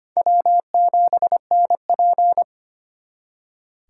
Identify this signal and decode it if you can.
Morse code, heard as a keyed tone, about 33 wpm, 700 Hz. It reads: W7NP